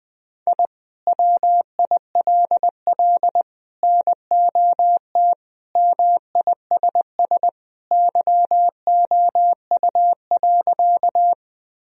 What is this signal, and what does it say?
Morse code, heard as a keyed tone, about 20 wpm, 700 Hz. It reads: I WILL NOT MISS YOU.
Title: Sentences from Top 1000 Words - Review Part 3 - 20wpm